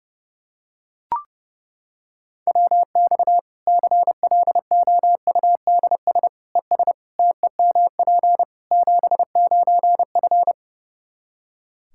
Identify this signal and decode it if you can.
Morse code, heard as a keyed tone, about 30 wpm, 700 Hz. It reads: WX CLOUDS ES TEMP 79F
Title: Practice QSO - Exchange 2 - Info - 30wpm